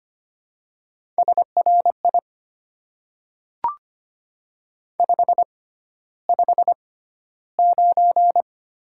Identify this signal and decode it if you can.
Morse code, heard as a keyed tone, about 25 wpm, spaced as for 13 wpm, 700 Hz. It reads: SRI 559
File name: Top 90 QSO Elements - 6x Character Spacing - 25wpm